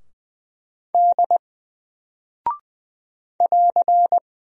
Morse code, heard as a keyed tone, about 20 wpm, 700 Hz. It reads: D <AR>
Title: Single Character (Letter:Number:Punctuation:Prosign) Alt 20wpm